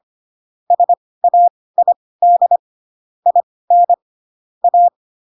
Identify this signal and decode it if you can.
Morse code, heard as a keyed tone, about 25 wpm, spaced as for 18 wpm, 700 Hz. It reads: SAID IN A
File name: Pinocchio ch3 12 25 ik1hge